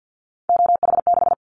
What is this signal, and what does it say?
Morse code, heard as a keyed tone, about 50 wpm, 700 Hz. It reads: OH6
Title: Callsign Prefixes for Top 50 DXCC - 50wpm